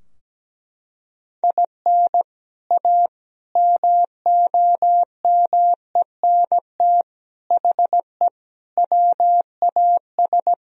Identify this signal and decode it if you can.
Morse code, heard as a keyed tone, about 17 wpm, 700 Hz. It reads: IN A MOMENT HE WAS
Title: Sentences from Top 1000 Words - Review Part 2 - 17wpm